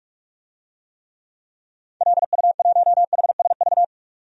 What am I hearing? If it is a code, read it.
Morse code, heard as a keyed tone, about 46 wpm, 700 Hz. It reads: LU1HSV
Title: DXCC-100 Call Signs - 6 Characters - 45wpm